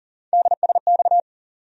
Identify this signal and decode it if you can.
Morse code, heard as a keyed tone, about 40 wpm, 700 Hz. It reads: DSX
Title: USA Call Sign Postfixes - 3 Letters 40wpm